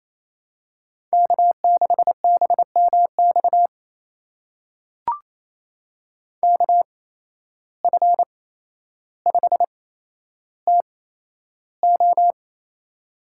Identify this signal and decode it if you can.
Morse code, heard as a keyed tone, about 28 wpm, spaced as for 12 wpm, 700 Hz. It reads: K6BMX KF5TO
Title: USA Call Signs - 5 Characters - 8x Character Spacing - 28wpm